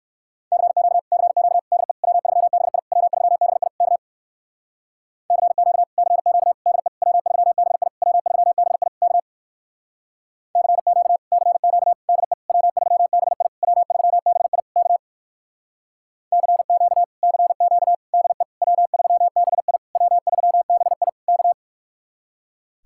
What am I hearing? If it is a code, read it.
Morse code, heard as a keyed tone, about 59 wpm, 700 Hz. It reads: CQ CQ DE W3BI W3BI K CQ CQ DE W3BI W3BI K CQ CQ DE W3BI W3BI K CQ CQ DE W3BI W3BI K